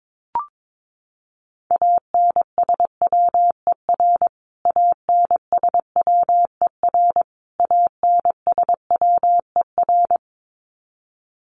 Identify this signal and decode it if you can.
Morse code, heard as a keyed tone, about 22 wpm, 700 Hz. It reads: ANSWER ANSWER ANSWER
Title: Top 500 Words Repeated 22wpm